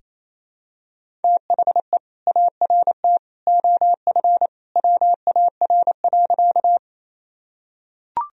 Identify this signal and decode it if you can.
Morse code, heard as a keyed tone, about 27 wpm, 700 Hz. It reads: THE ART OF WAR.
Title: Sentences from Top 1000 Words - Review Part 3 - 28wpm